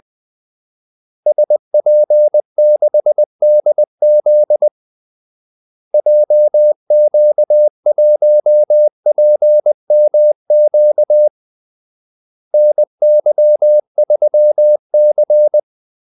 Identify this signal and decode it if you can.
Morse code, heard as a keyed tone, about 20 wpm, 600 Hz. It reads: SP6DZ JQ1PMQ NY3C
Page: Kaldesignaler 20 wpm